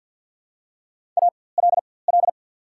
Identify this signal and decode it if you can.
Morse code, heard as a keyed tone, about 48 wpm, spaced as for 23 wpm, 700 Hz. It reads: ALL
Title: 4x Character Spacing - Top 200 Words - 50wpm